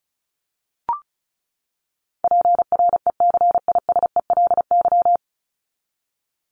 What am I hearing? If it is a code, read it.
Morse code, heard as a keyed tone, about 34 wpm, 700 Hz. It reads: PRECISELY
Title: LY Suffix - Top 100 Words with Base Words 35wpm